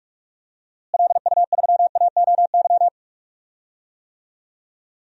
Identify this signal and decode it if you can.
Morse code, heard as a keyed tone, about 41 wpm, 700 Hz. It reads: LU3AOY
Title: DXCC-100 Call Signs - 6 Characters - Repeated 3x - 45wpm